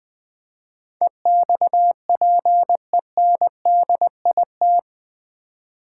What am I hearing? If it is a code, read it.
Morse code, heard as a keyed tone, about 20 wpm, 700 Hz. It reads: EXPENDIT
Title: Build A Word - 4 Letters At A Time - 11-Letter Words - 20wpm